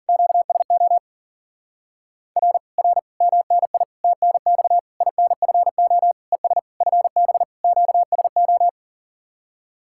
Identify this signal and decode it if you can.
Morse code, heard as a keyed tone, about 52 wpm, 700 Hz. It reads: QSO R R MNI TNX INFO ES FB QSO